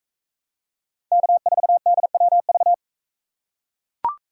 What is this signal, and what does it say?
Morse code, heard as a keyed tone, about 44 wpm, 700 Hz. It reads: K4DWV